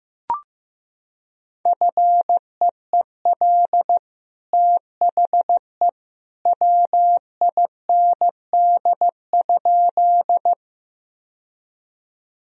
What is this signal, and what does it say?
Morse code, heard as a keyed tone, about 15 wpm, 700 Hz. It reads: FEEL THE WIND?
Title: Sentences from Top 1000 Words - Review Part 2 - 15wpm